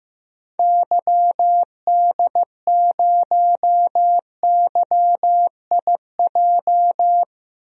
Morse code, heard as a keyed tone, about 15 wpm, 700 Hz. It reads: YD0YIJ
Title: DXCC-100 Call Signs - 6 Characters - Repeated 3x - 15wpm